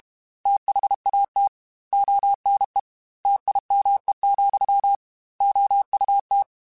Morse code, heard as a keyed tone, about 32 wpm, 800 Hz. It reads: THAT ONE TIME, OUT
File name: Technical Subjects / Technical Subjects-mp3s / AND Gate @30